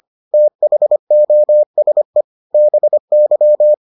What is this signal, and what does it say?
Morse code, heard as a keyed tone, about 25 wpm, 600 Hz. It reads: THOSE BY